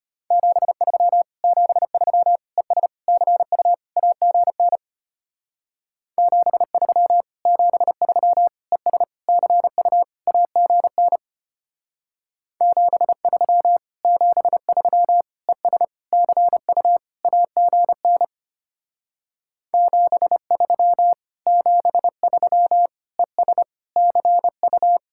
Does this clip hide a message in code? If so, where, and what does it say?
Yes: Morse code from 0.3 to 25.0 s, heard as a keyed tone, about 33 wpm, 700 Hz. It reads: 73 73 ES CU AGN 73 73 ES CU AGN 73 73 ES CU AGN 73 73 ES CU